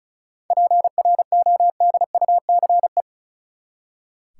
Morse code, heard as a keyed tone, about 34 wpm, 700 Hz. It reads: PRODUCE
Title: Seven Letter Words - 35wpm